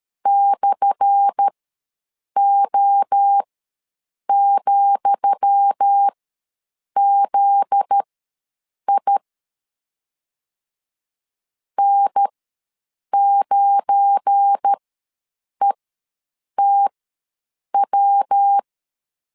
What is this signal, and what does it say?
Morse code, heard as a keyed tone, about 13 wpm, spaced as for 7 wpm, 800 Hz. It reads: /O,ZI N9ETW
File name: LEZIONI / 10 SIGNS / 10D COMA ws 12wpm 3618